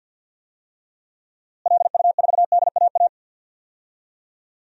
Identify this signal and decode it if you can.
Morse code, heard as a keyed tone, about 52 wpm, 700 Hz. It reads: LU4DAA
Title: DXCC-100 Call Signs - 6 Characters - Repeated 3x - 50wpm